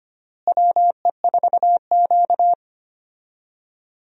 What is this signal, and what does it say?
Morse code, heard as a keyed tone, about 25 wpm, 700 Hz. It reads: WE4Q
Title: USA Call Signs - 4 Characters - Repeated 3x - 25wpm